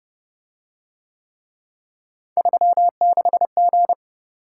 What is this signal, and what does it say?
Morse code, heard as a keyed tone, about 30 wpm, 700 Hz. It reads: 36G